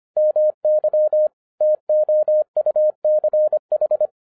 Morse code, heard as a keyed tone, about 25 wpm, 610 Hz. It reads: MY TOUCH
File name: FIRST FISTS CONTACT-25wpm